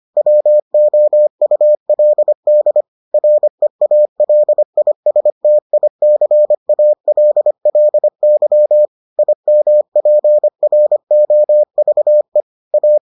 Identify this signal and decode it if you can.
Morse code, heard as a keyed tone, about 25 wpm, 600 Hz. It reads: WOULD REALISTICALLY IMPROVE A